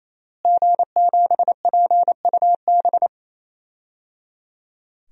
Morse code, heard as a keyed tone, about 28 wpm, 700 Hz. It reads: G7PUB
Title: DXCC-100 Call Signs - 5 Characters - Repeated 3x - 28wpm